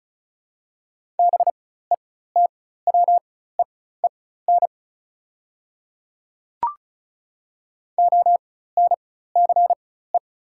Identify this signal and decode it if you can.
Morse code, heard as a keyed tone, about 33 wpm, spaced as for 16 wpm, 690 Hz. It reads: BETWEEN ONCE
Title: 4x Character Spacing - Top 300 Words - 35wpm